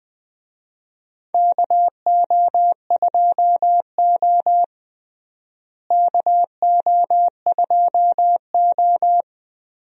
Morse code, heard as a keyed tone, about 20 wpm, 700 Hz. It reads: KO2O KO2O
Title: USA Call Signs - 4 Characters - Repeated 3x - 20wpm